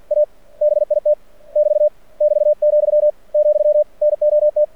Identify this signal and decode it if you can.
Morse code, heard as a keyed tone, about 53 wpm, 600 Hz. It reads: A BIT <BT> <BT>, , NOT
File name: archived / CRSnet